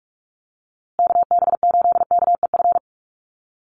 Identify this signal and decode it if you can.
Morse code, heard as a keyed tone, about 44 wpm, 700 Hz. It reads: KB8KEF
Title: USA Call Signs Repeated 3x 45wpm